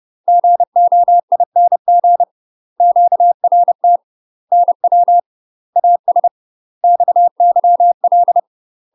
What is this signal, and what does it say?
Morse code, heard as a keyed tone, about 30 wpm, 700 Hz. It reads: GOING QRT NW AS XYL
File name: UK MASTER QSOs TRAINING / 30WPM / UK MASTER QSOs 0047